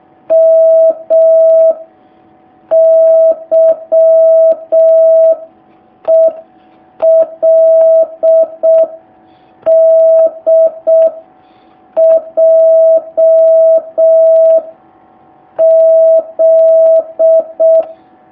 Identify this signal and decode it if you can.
Morse code, heard as a keyed tone, about 6 wpm, 650 Hz. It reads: M Y E L D J Z
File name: jh8cbh / 2019cw